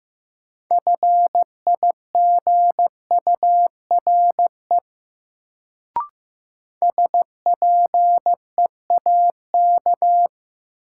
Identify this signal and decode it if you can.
Morse code, heard as a keyed tone, about 15 wpm, 700 Hz. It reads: FIGURE SPEAK
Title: Sentences from Top 1000 Words - Review Part 2 - New Word Drill - 15wpm